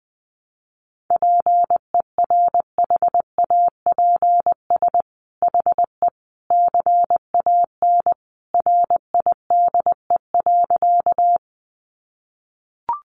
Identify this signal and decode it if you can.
Morse code, heard as a keyed tone, about 20 wpm, 700 Hz. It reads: PERHAPS HE CAN RIDE.
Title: Sentences from Top 1200 Words 20wpm